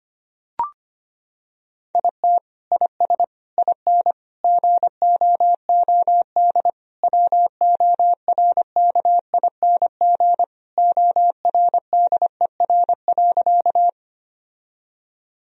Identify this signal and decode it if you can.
Morse code, heard as a keyed tone, about 25 wpm, 700 Hz. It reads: IT IS IN GOOD WORKING ORDER.
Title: Sentences from Top 1000 Words - Review Part 4 - 25wpm